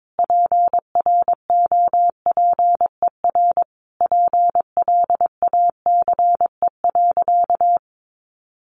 Morse code, heard as a keyed tone, about 22 wpm, 700 Hz. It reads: PROPER PLACE.
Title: Sentences from Top 800 Words 22wpm